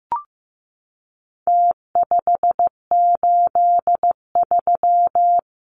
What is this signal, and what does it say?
Morse code, heard as a keyed tone, about 15 wpm, 700 Hz. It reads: T583
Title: Numbers - Contest Style - 4-Digit 15wpm